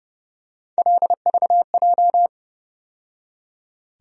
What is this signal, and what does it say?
Morse code, heard as a keyed tone, about 30 wpm, 700 Hz. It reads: LVJ
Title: USA Call Sign Postfixes - 3 Letters 30wpm